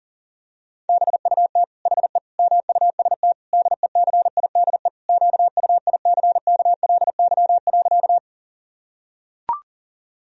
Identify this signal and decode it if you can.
Morse code, heard as a keyed tone, about 40 wpm, 700 Hz. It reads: BUT HE MUST DECIDE QUICKLY.
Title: Sentences from Top 1000 Words - Review Part 3 - 40wpm